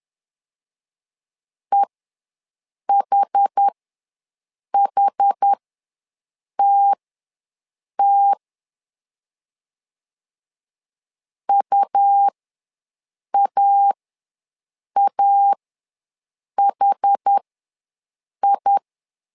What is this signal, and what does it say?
Morse code, heard as a keyed tone, about 10 wpm, spaced as for 6 wpm, 800 Hz. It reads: EHHTT UAAHI
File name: LEZIONI / 2 UHDN / 2 UH ws 10wpm 2029